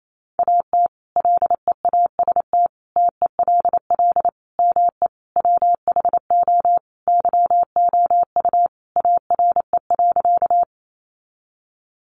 Morse code, heard as a keyed tone, about 28 wpm, 700 Hz. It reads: AT LEAST TELL ME WHO YOU ARE.
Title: Sentences from Top 700 Words 28wpm